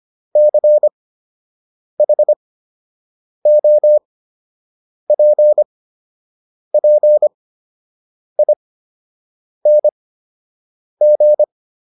Morse code, heard as a keyed tone, about 25 wpm, spaced as for 7 wpm, 600 Hz. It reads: CHOPPING